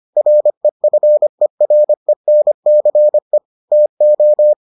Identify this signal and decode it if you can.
Morse code, heard as a keyed tone, about 25 wpm, 600 Hz. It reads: REFERENCE TO